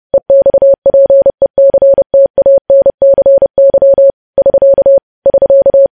Morse code, heard as a keyed tone, about 30 wpm, 550 Hz. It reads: EXPECTANCY <SK> <SK>
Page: news headlines in morse code at 30 wpm